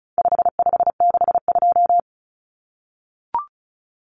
Morse code, heard as a keyed tone, about 34 wpm, 700 Hz. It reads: 5562
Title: Common Cut Numbers - 4-Digit 35wpm